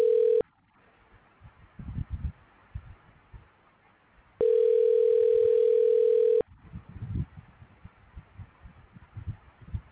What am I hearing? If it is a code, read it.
Ringback tone (North America), 440 + 480 Hz.